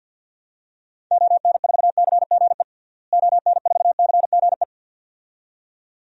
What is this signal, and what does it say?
Morse code, heard as a keyed tone, about 47 wpm, 700 Hz. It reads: ON4CGE ON4CGE
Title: DXCC-100 Call Signs - 6 Characters - Repeated 2x - 50wpm